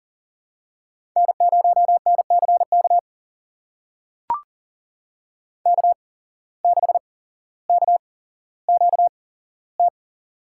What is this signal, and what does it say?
Morse code, heard as a keyed tone, about 40 wpm, spaced as for 17 wpm, 700 Hz. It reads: N0NCK K6KQT
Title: USA Call Signs - 8x Character Spacing - 40wpm